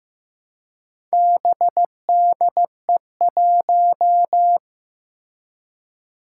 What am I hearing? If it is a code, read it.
Morse code, heard as a keyed tone, about 16 wpm, 700 Hz. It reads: BDE1
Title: Four Challenging Letter-Number Combinations 15wpm